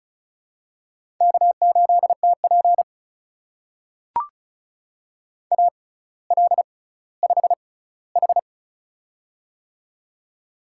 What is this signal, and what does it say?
Morse code, heard as a keyed tone, about 36 wpm, spaced as for 18 wpm, 700 Hz. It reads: K8TP AL5H
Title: USA Call Signs - 6x Character Spacing - 35wpm